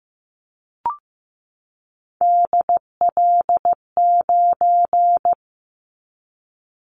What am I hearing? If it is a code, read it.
Morse code, heard as a keyed tone, about 15 wpm, 700 Hz. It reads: DL9
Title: Callsign Prefixes for Top 25 DXCC - 15wpm